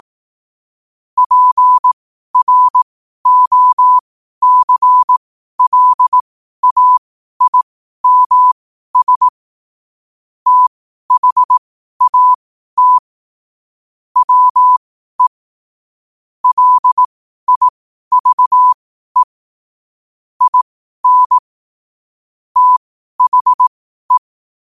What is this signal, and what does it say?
Morse code, heard as a keyed tone, about 18 wpm, spaced as for 12 wpm, 1000 Hz. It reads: PROCLAIMS THAT WE LIVE IN THE
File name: qotdfiles / 12wpm